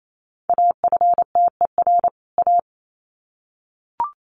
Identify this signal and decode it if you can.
Morse code, heard as a keyed tone, about 28 wpm, 700 Hz. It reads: AFTER A
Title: Sets of 2 Words 28wpm